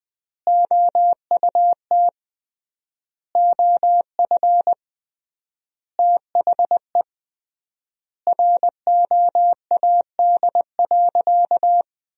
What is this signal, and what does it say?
Morse code, heard as a keyed tone, about 21 wpm, 700 Hz. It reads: OUT OF THE ROAD.
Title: 3x Word-Spacing - Sentences from Top 400 Words - 20wpm